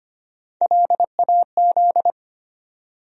Morse code, heard as a keyed tone, about 25 wpm, 700 Hz. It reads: LAZ
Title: USA Call Sign Postfixes - 3 Letters 25wpm